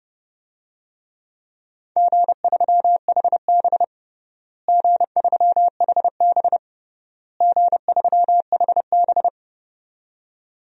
Morse code, heard as a keyed tone, about 30 wpm, 700 Hz. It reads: G3HB G3HB G3HB
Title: DXCC-100 Call Signs - 4 Characters - Repeated 3x - 30wpm